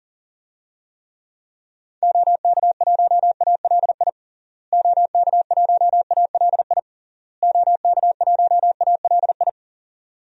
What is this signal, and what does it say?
Morse code, heard as a keyed tone, about 40 wpm, 700 Hz. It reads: OK1ALI OK1ALI OK1ALI
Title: DXCC-100 Call Signs - 6 Characters - Repeated 3x - 40wpm